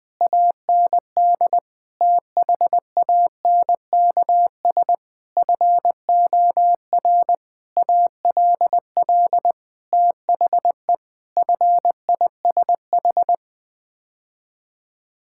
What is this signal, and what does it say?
Morse code, heard as a keyed tone, about 20 wpm, 700 Hz. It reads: AND THANKS FOR ALL THE FISH
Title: Sets of 8 Words - 20wpm